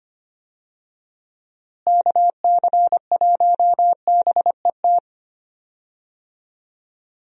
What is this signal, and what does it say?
Morse code, heard as a keyed tone, about 25 wpm, 700 Hz. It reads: KC1BET
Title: USA Call Signs - 6 Characters - Repeated 2x - 25wpm